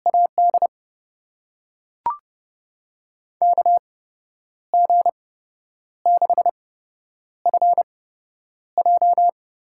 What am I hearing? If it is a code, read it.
Morse code, heard as a keyed tone, about 30 wpm, spaced as for 10 wpm, 700 Hz. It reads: AD KG6FJ
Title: USA Call Signs - 6 Characters - 8x Character Spacing - 30wpm